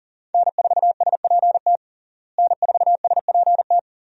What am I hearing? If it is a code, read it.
Morse code, heard as a keyed tone, about 37 wpm, 700 Hz. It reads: N4SPT N4SPT
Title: USA Call Signs - 5 Characters - Repeated 2x - 40wpm